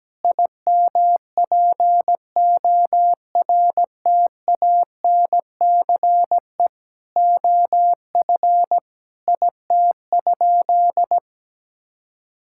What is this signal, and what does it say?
Morse code, heard as a keyed tone, about 17 wpm, 700 Hz. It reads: IMPORTANCE OF IT?
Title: Sentences from Top 1600 Words 17wpm